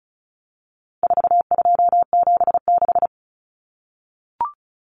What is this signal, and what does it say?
Morse code, heard as a keyed tone, about 34 wpm, 700 Hz. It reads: 4276
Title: Common Cut Numbers - 4-Digit 35wpm